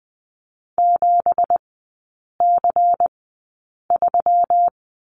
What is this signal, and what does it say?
Morse code, heard as a keyed tone, about 20 wpm, spaced as for 8 wpm, 700 Hz. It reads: 7C3
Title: 198 Lesson 32e - Letter k - Three Characters 20wpm